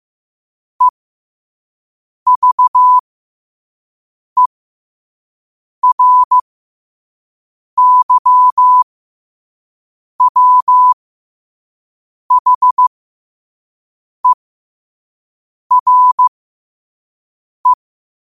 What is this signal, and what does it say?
Morse code, heard as a keyed tone, about 15 wpm, spaced as for 5 wpm, 1000 Hz. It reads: EVERYWHERE